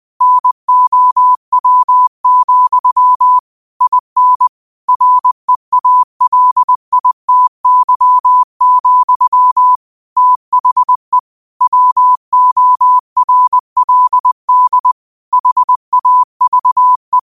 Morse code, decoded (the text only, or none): NOW, IN REALITY, THE WORLD HAVE